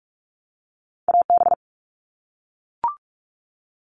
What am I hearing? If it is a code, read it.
Morse code, heard as a keyed tone, about 44 wpm, 700 Hz. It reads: AB
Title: Canadian Provinces and Territories - Repeated 3x - 45wpm